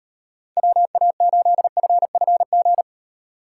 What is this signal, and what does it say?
Morse code, heard as a keyed tone, about 38 wpm, 700 Hz. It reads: WA8FFG